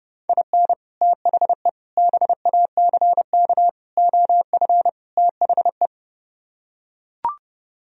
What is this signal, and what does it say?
Morse code, heard as a keyed tone, about 30 wpm, 700 Hz. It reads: IN THE BACK OF THE